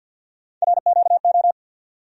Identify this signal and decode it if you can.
Morse code, heard as a keyed tone, about 47 wpm, 700 Hz. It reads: RQO
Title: USA Call Sign Postfixes - 3 Letters 50wpm